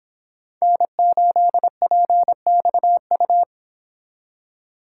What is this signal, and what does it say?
Morse code, heard as a keyed tone, about 26 wpm, 700 Hz. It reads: N8PXU